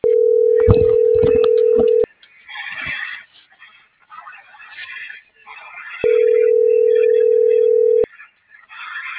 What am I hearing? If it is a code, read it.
Ringback tone (North America), 440 + 480 Hz.